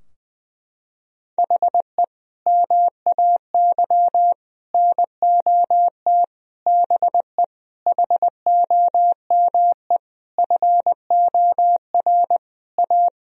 Morse code, heard as a keyed tone, about 20 wpm, 700 Hz. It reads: HE MAY NOT BE HOME FOR A
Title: Sentences from Top 1000 Words - Review Part 2 - 20wpm